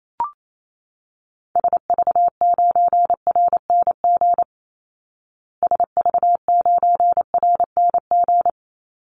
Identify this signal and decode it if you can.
Morse code, heard as a keyed tone, about 28 wpm, 700 Hz. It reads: SV9RNG SV9RNG
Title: Call Signs Repeated 3x - 28wpm